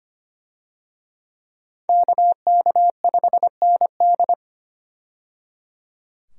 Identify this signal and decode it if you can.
Morse code, heard as a keyed tone, about 25 wpm, 700 Hz. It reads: KK5ND
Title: USA Call Signs - 5 Characters - 25wpm